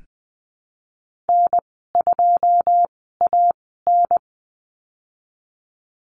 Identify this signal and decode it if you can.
Morse code, heard as a keyed tone, about 20 wpm, spaced as for 15 wpm, 700 Hz. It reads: N2AN
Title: 088 Lesson 15g - Number 5 - Call Signs 20wpm